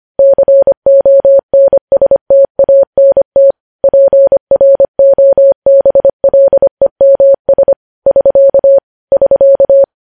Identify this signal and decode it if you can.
Morse code, heard as a keyed tone, about 25 wpm, 550 Hz. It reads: CONSTANT PROBLEMS <SK> <SK>
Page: news-headlines-in-morse-code-at-25-wpm